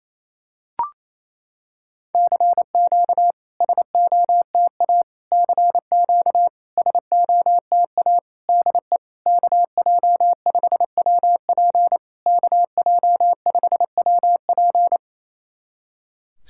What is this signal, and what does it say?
Morse code, heard as a keyed tone, about 28 wpm, 700 Hz. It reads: CQ SOTA CQ SOTA DE KJ5WP KJ5WP
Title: SOTA QSO Practice 28wpm